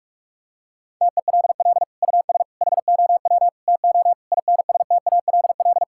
Morse code, heard as a keyed tone, about 46 wpm, 700 Hz. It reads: TELL US HOW TO INSTALL
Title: Sentences from Top 1600 Words 45wpm